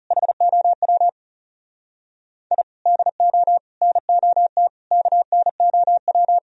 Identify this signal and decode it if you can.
Morse code, heard as a keyed tone, about 40 wpm, 700 Hz. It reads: HOW I DO NOT KNOW